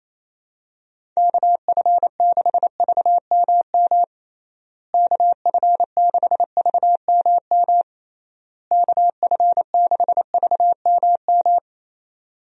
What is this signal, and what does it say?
Morse code, heard as a keyed tone, about 28 wpm, 700 Hz. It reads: KF6VMM KF6VMM KF6VMM
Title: USA Call Signs - 6 Characters - Repeated 3x - 28wpm